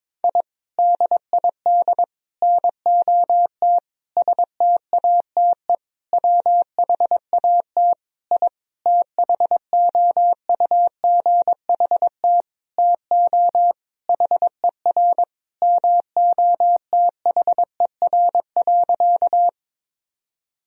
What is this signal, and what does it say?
Morse code, heard as a keyed tone, about 22 wpm, 700 Hz. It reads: I DID NOT STATE WHAT I THOUGHT TO HER MOTHER.
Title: Sentences from Top 1000 Words - Review Part 2 - 22wpm